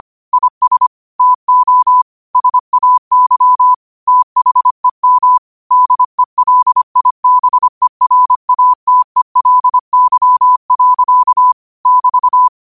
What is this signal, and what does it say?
Morse code, heard as a keyed tone, about 25 wpm, 1000 Hz. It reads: IS TO SAY THEM DELIBERATELY. <BT>
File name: qotdfiles / 25wpm